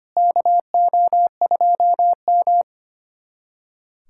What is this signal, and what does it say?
Morse code, heard as a keyed tone, about 25 wpm, 700 Hz. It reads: KO2M
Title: USA Callsign DE Callsign - 4 Characters - 25wpm